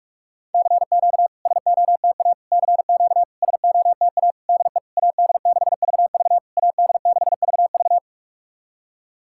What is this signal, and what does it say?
Morse code, heard as a keyed tone, about 46 wpm, 690 Hz. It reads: CQ SOTA CQ SOTA DE AD6VV AD6VV
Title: SOTA QSO Practice 45wpm